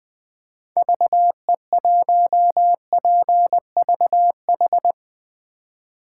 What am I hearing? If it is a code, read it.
Morse code, heard as a keyed tone, about 21 wpm, 700 Hz. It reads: VE1PVH
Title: 2023 SOTA Chaser Call Signs - 20wpm